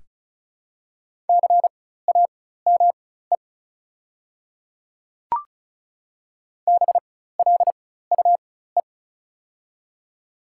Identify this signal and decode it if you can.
Morse code, heard as a keyed tone, about 34 wpm, spaced as for 16 wpm, 690 Hz. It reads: CAME BLUE
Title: 4x Character Spacing - Top 500 Words - 35wpm